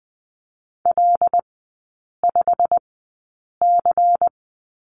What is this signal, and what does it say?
Morse code, heard as a keyed tone, about 20 wpm, spaced as for 8 wpm, 690 Hz. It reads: L5C
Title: 093 Lesson 16e - Letter c - Three Characters 20wpm